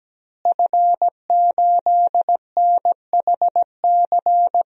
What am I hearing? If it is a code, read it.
Morse code, heard as a keyed tone, about 17 wpm, 700 Hz. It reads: F8NHC
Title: DXCC-100 Call Signs - 5 Characters - Repeated 3x - 17wpm